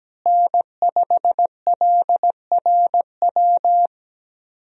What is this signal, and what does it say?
Morse code, heard as a keyed tone, about 17 wpm, 700 Hz. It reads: N5LRW
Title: USA Call Signs - 5 Characters - Repeated 2x - 17wpm